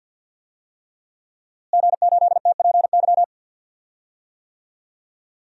Morse code, heard as a keyed tone, about 50 wpm, 700 Hz. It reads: G8TPY
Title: DXCC-100 Call Signs - 5 Characters - Repeated 3x - 50wpm